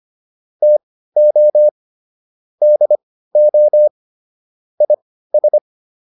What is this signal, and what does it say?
Morse code, heard as a keyed tone, about 25 wpm, spaced as for 15 wpm, 600 Hz. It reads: TO DO IS